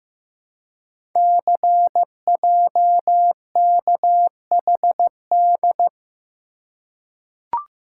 Morse code, heard as a keyed tone, about 15 wpm, 700 Hz. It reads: CJKHD
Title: Code Groups - 5 Letter:Numbers - 15wpm